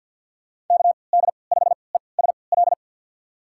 Morse code, heard as a keyed tone, about 47 wpm, spaced as for 28 wpm, 700 Hz. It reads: KD5ESL
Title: USA Call Signs - 6 Characters - 3x Character Spacing - 50wpm